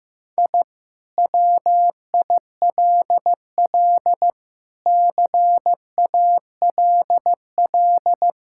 Morse code, heard as a keyed tone, about 16 wpm, 700 Hz. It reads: I WILL CALL
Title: Sentences from Top 1000 Words - Review Part 6 - 15wpm